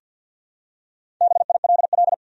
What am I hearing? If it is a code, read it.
Morse code, heard as a keyed tone, about 53 wpm, 700 Hz. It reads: BILL
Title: USA Common Names - Max 4 Letters - 50wpm